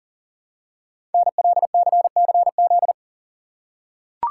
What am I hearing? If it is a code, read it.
Morse code, heard as a keyed tone, about 40 wpm, 690 Hz. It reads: NLCCZ
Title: Code Groups - 5 Letters - 40wpm